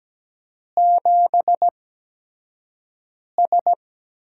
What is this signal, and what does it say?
Morse code, heard as a keyed tone, about 17 wpm, spaced as for 5 wpm, 700 Hz. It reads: 7S
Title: USA Call Signs - 8x Character Spacing - 17wpm